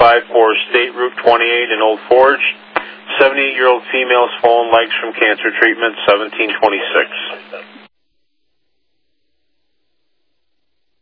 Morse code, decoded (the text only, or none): none